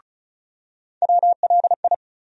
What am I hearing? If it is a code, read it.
Morse code, heard as a keyed tone, about 34 wpm, 700 Hz. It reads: WLI
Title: USA Call Sign Postfixes - 3 Letters 35wpm